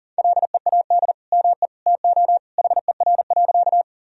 Morse code, heard as a keyed tone, about 40 wpm, 700 Hz. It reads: LEAD ME TO HER.
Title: Sentences from Top 1000 Words - Review Part 2 - 40wpm